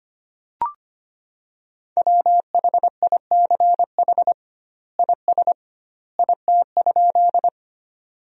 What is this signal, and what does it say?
Morse code, heard as a keyed tone, about 25 wpm, 700 Hz. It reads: WHICH IS IT?
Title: 2x Word-Spacing - Sentences from Top 100 Words - 25wpm